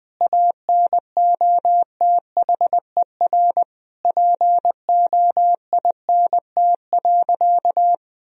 Morse code, heard as a keyed tone, about 20 wpm, 700 Hz. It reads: ANOTHER POINT.
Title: Sentences from Top 1000 Words - Review Part 4 - 20wpm